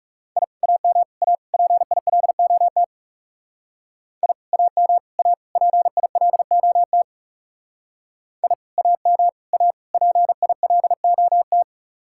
Morse code, heard as a keyed tone, about 38 wpm, 700 Hz. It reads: I AM A PILOT I AM A PILOT I AM A PILOT